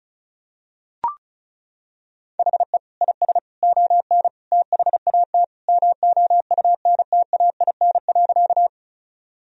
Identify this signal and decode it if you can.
Morse code, heard as a keyed tone, about 36 wpm, 690 Hz. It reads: HE IS ON THAT MOUNTAIN.
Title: Sentences from Top 1000 Words - Review Part 1 - 35wpm